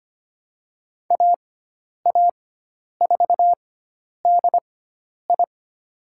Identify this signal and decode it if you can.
Morse code, heard as a keyed tone, about 25 wpm, spaced as for 10 wpm, 700 Hz. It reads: AA4DI
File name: Lesson 16g - Letter c - Call Signs 25wpm